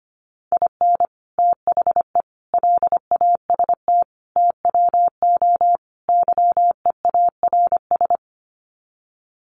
Morse code, heard as a keyed tone, about 25 wpm, 700 Hz. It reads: IN THE LAST TWO YEARS
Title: Sets of 5 Words 25wpm